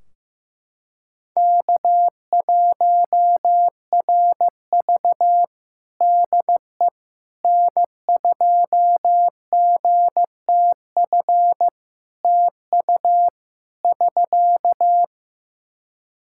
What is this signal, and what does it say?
Morse code, heard as a keyed tone, about 15 wpm, 700 Hz. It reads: K1RV DE N2GTF TU <SK>
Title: Practice QSO - Exchange 4 - Ending - 15wpm